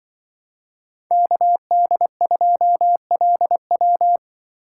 Morse code, heard as a keyed tone, about 24 wpm, 700 Hz. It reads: KD2LW
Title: Speed Racing - USA Call Signs 20wpm